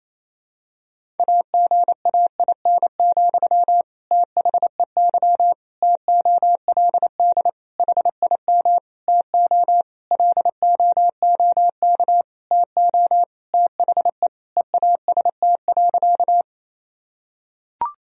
Morse code, heard as a keyed tone, about 28 wpm, 700 Hz. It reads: AGAIN, THEY TOLD HIM TO LOOK TO THE EAST.
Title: Sentences from Top 1000 Words - Review Part 3 - 28wpm